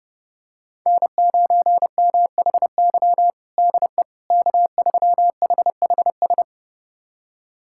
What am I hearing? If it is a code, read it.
Morse code, heard as a keyed tone, about 30 wpm, 700 Hz. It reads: N9MHY DE K3HHS
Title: USA Callsign DE Callsign - 5 Characters - 30wpm